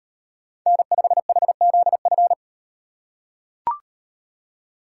Morse code, heard as a keyed tone, about 38 wpm, 700 Hz. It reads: N5HZF